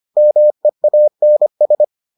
Morse code, heard as a keyed tone, about 25 wpm, 600 Hz. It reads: MEANS